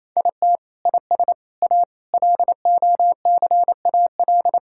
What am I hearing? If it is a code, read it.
Morse code, heard as a keyed tone, about 28 wpm, 700 Hz. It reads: IT IS A LOCAL
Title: Sentences from Top 1000 Words - Review Part 5 - 28wpm